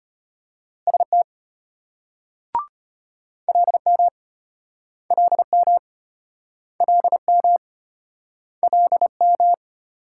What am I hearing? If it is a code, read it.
Morse code, heard as a keyed tone, about 31 wpm, 700 Hz. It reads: ST LM LM LM LM
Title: Speed-Racing Multi-letter Phonemes - 25wpm